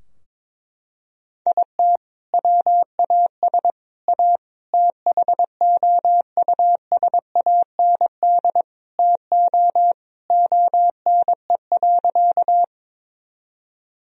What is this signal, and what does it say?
Morse code, heard as a keyed tone, about 21 wpm, 700 Hz. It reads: IT WAS A THOUSAND TO ONE.
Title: Sentences from Top 1000 Words - Review Part 3 - 22wpm